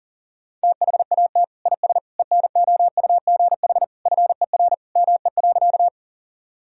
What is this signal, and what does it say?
Morse code, heard as a keyed tone, about 40 wpm, 700 Hz. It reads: THAT IS ENOUGH FER ME.
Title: Sentences from Top 400 Words and Abbreviations 40wpm